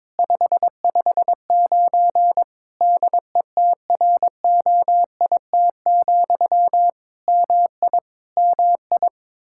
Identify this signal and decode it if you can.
Morse code, heard as a keyed tone, about 22 wpm, 700 Hz. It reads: 559 DETROIT, MI MI
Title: Practice QSO - Exchange 1 - Beginning - 22wpm